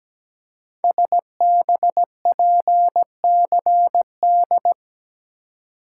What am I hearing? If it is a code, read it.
Morse code, heard as a keyed tone, about 17 wpm, 700 Hz. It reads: SBPCD
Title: Code Groups - 5 Letters - 17wpm